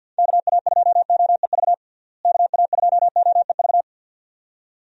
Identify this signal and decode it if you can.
Morse code, heard as a keyed tone, about 52 wpm, 700 Hz. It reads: KA2OEV KA2OEV
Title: USA Call Signs - 6 Characters - Repeated 2x - 50wpm